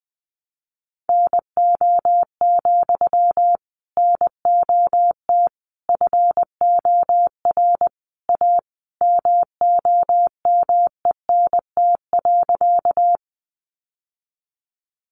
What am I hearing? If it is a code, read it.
Morse code, heard as a keyed tone, about 20 wpm, 700 Hz. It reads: NO, NOT FOR A MOMENT.
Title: Sentences from Top 1000 Words 20wpm